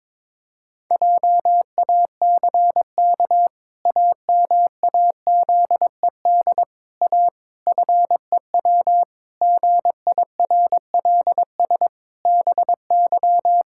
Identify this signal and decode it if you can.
Morse code, heard as a keyed tone, about 22 wpm, 700 Hz. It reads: JACK AMAZED A FEW GIRLS BY
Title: Pangram Challenge v2 - 22wpm